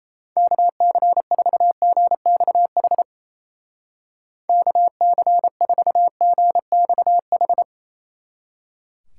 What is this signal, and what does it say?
Morse code, heard as a keyed tone, about 37 wpm, 700 Hz. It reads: KC4GXH KC4GXH